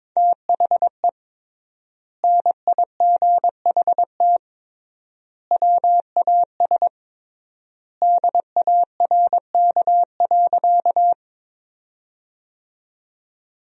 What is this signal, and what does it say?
Morse code, heard as a keyed tone, about 22 wpm, 700 Hz. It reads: THE NIGHT WAS DARK.
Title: 3x Word-Spacing - Sentences from Top 500 Words - 22wpm